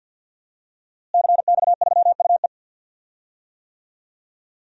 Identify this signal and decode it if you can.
Morse code, heard as a keyed tone, about 51 wpm, 700 Hz. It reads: CX3UE
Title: DXCC-100 Call Signs - 5 Characters - Repeated 3x - 50wpm